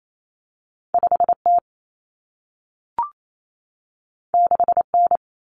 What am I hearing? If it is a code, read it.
Morse code, heard as a keyed tone, about 28 wpm, 700 Hz. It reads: 5T 6N
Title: Common Cut Numbers - 2-Digit 28wpm